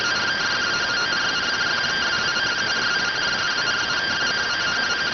Two-tone teleprinter signal, about 170 Hz shift.